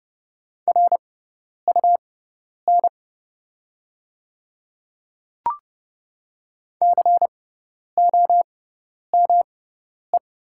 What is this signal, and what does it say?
Morse code, heard as a keyed tone, about 29 wpm, spaced as for 10 wpm, 690 Hz. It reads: RUN COME
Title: 6x Character Spacing - Top 300 Words - 30wpm